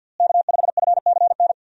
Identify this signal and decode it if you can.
Morse code, heard as a keyed tone, about 52 wpm, 700 Hz. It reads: K5FCN
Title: USA Call Signs - 5 Characters - Repeated 3x - 50wpm